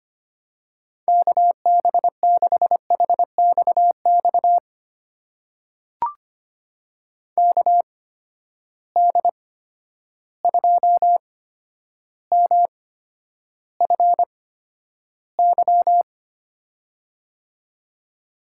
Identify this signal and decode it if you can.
Morse code, heard as a keyed tone, about 25 wpm, spaced as for 11 wpm, 700 Hz. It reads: KB6HXX KD2MFY
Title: USA Call Signs - 6 Characters - 8x Character Spacing - 25wpm